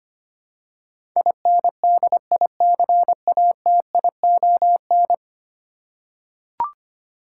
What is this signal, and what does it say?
Morse code, heard as a keyed tone, about 25 wpm, 700 Hz. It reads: INDICATION
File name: 10 Letter Words - First 3 Letters Given - 25wpm